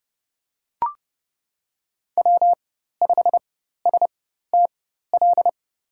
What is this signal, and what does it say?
Morse code, heard as a keyed tone, about 32 wpm, spaced as for 14 wpm, 700 Hz. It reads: W5STL
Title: USA Call Signs - 5 Characters - 4x Character Spacing - 30wpm